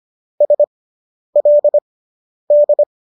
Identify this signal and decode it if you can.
Morse code, heard as a keyed tone, about 25 wpm, spaced as for 10 wpm, 590 Hz. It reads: SLD